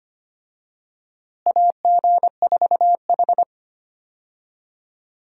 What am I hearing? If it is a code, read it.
Morse code, heard as a keyed tone, about 25 wpm, 700 Hz. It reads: AG4H
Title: USA Call Signs - 4 Characters - Repeated 2x - 25wpm